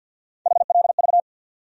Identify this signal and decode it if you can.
Morse code, heard as a keyed tone, about 52 wpm, 700 Hz. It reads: HLV
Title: USA Call Sign Postfixes - 3 Letters 50wpm